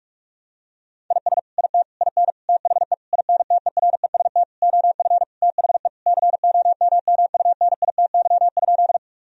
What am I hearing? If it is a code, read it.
Morse code, heard as a keyed tone, about 42 wpm, 700 Hz. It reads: IS IT IN THE INTEREST OF THE COMMUNITY?
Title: Sentences from Top 1000 Words - Review Part 4 - 45wpm